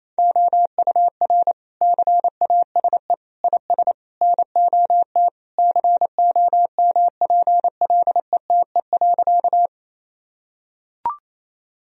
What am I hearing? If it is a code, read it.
Morse code, heard as a keyed tone, about 28 wpm, 700 Hz. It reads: OUR CASE IS NOT COMPLETE.
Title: Sentences from Top 1000 Words - Review Part 2 - 28wpm